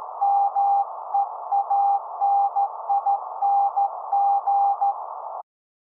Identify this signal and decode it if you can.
Morse code, heard as a keyed tone, about 14 wpm, 800 Hz. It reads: MEANING
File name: se / pre2 / Morse Code02-4(Slow-Noise)